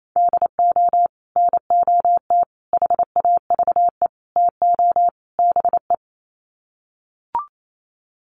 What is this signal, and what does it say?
Morse code, heard as a keyed tone, about 28 wpm, 700 Hz. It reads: DO NOT HAVE TO BE
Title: Sets of 5 Words 28wpm